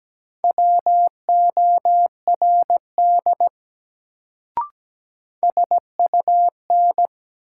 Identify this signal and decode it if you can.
Morse code, heard as a keyed tone, about 17 wpm, 700 Hz. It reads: WORD SUN
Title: Sentences from Top 300 Words - New Word Drill - 17wpm